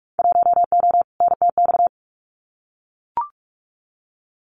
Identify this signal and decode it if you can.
Morse code, heard as a keyed tone, about 44 wpm, 700 Hz. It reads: 1O NTX
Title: Winter Field Day Exchanges 45wpm